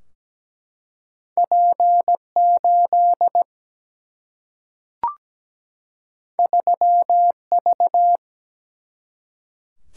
Morse code, heard as a keyed tone, about 17 wpm, 700 Hz. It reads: P 8 3 V
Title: Single-Letter Single-Number 17wpm